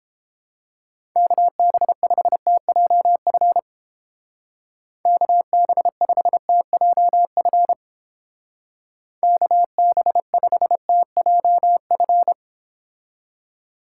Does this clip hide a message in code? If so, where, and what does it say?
Yes: Morse code from 1.2 to 12.3 s, heard as a keyed tone, about 29 wpm, 700 Hz. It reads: KB5TJF KB5TJF KB5TJF